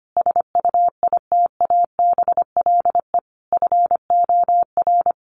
Morse code, heard as a keyed tone, about 25 wpm, 700 Hz. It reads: SUITABLE FOR
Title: Sentences from Top 1400 Words 25wpm